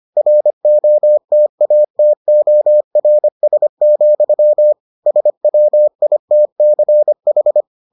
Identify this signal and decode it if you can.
Morse code, heard as a keyed tone, about 25 wpm, 600 Hz. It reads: ROTATORS, SWITCH